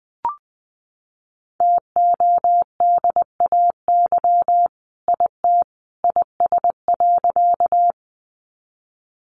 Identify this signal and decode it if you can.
Morse code, heard as a keyed tone, about 20 wpm, 700 Hz. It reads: TODAY IT IS.
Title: Sentences from Top 600 Words 20wpm